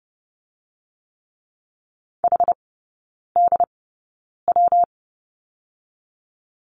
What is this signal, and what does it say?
Morse code, heard as a keyed tone, about 30 wpm, spaced as for 9 wpm, 700 Hz. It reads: HDW
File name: Lesson 19e - Letter w - Three Characters 30wpm